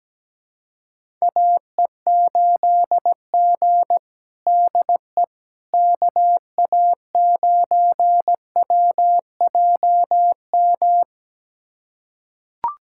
Morse code, heard as a keyed tone, about 17 wpm, 700 Hz. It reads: AE8G DE KA9WJM
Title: USA Callsign DE Callsign 17wpm